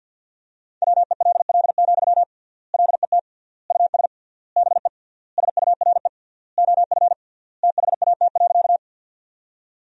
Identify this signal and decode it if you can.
Morse code, heard as a keyed tone, about 50 wpm, 700 Hz. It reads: WELL, LET US BE SURE OF THAT.
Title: 3x Word-Spacing - Sentences from Top 300 Words - 50wpm